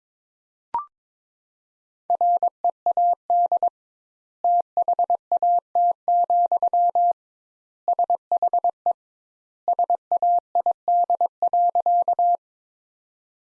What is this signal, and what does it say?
Morse code, heard as a keyed tone, about 22 wpm, 700 Hz. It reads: READ THAT, SHE SAID.
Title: Sentences Top 200 Words - 22wpm